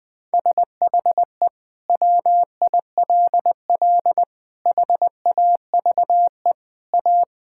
Morse code, heard as a keyed tone, about 20 wpm, 700 Hz. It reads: SHE WILL HAVE A
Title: Sentences from Top 1000 Words - Review Part 6 - 20wpm